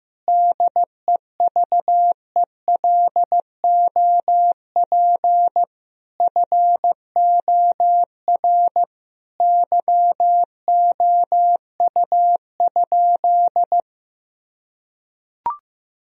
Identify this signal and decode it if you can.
Morse code, heard as a keyed tone, about 16 wpm, 700 Hz. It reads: DEVELOP FOR YOU?
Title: Sentences from Top 1000 Words - Review Part 2 - 15wpm